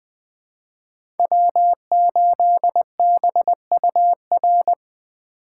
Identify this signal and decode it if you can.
Morse code, heard as a keyed tone, about 20 wpm, 700 Hz. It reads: W8BUR
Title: USA Call Signs - 5 Characters - 3x Character Spacing - 20wpm